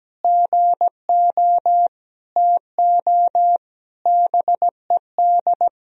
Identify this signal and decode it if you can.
Morse code, heard as a keyed tone, about 17 wpm, 700 Hz. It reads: GO TO BED
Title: Sentences from Top 1000 Words - Review Part 4 - 17wpm